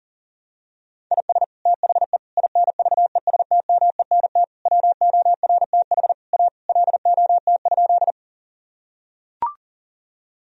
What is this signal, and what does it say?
Morse code, heard as a keyed tone, about 40 wpm, 700 Hz. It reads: IS THE INVESTMENT WORTH A LOT?
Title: Sentences from Top 1000 Words - Review Part 4 - 40wpm